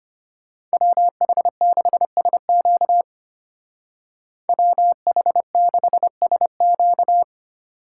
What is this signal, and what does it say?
Morse code, heard as a keyed tone, about 31 wpm, 700 Hz. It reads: WH6SQ WH6SQ
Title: Speed Racing - USA Call Signs 25wpm